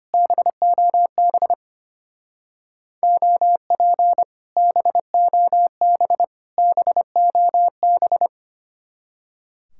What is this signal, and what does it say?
Morse code, heard as a keyed tone, about 30 wpm, 700 Hz. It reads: BOB OP BOB BOB
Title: Speed-Racing Common QSO Phrases - 25wpm